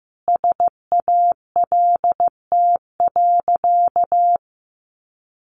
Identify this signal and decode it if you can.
Morse code, heard as a keyed tone, about 15 wpm, 700 Hz. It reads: SALT.
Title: Sentences from Top 800 Words 15wpm